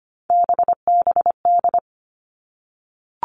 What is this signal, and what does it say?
Morse code, heard as a keyed tone, about 25 wpm, 700 Hz. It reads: BBD
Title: All Cut Numbers - 3-Digit 25wpm